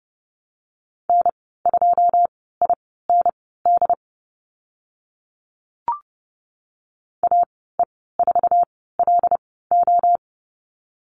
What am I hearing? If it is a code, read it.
Morse code, heard as a keyed tone, about 30 wpm, spaced as for 17 wpm, 700 Hz. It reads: N2IND AE4LO
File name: Lesson 14g - Number 2 - Call Signs 30wpm